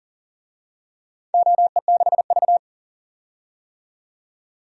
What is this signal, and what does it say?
Morse code, heard as a keyed tone, about 40 wpm, 700 Hz. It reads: OE6V
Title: DXCC-100 Call Signs - 4 Characters - 40wpm